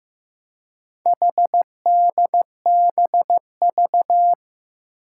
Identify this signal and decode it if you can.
Morse code, heard as a keyed tone, about 15 wpm, 700 Hz. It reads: HDBV
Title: Four Challenging Letter-Number Combinations 15wpm